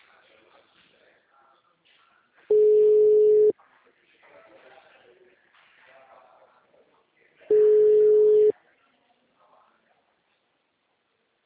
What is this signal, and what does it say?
Ringback tone (Europe, 425 Hz), with speech.